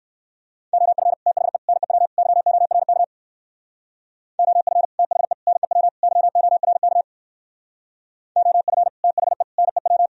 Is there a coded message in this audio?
Yes: Morse code from 0.7 to 10.1 s, heard as a keyed tone, about 57 wpm, 700 Hz. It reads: OF THE NEW YORK OF THE NEW YORK OF THE NEW